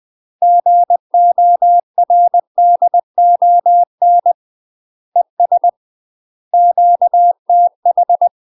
Morse code, heard as a keyed tone, about 20 wpm, 700 Hz. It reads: GORDON ES QTH
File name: UK MASTER QSOs TRAINING / 20WPM / UK MASTER QSOs 0002